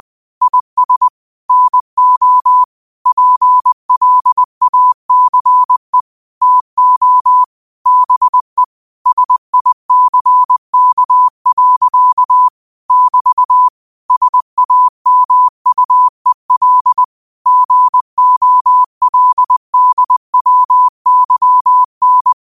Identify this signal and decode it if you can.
Morse code, heard as a keyed tone, about 20 wpm, 1000 Hz. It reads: IS NO PLACE TO BE SICK. <BT> SAMUEL GOLDWYN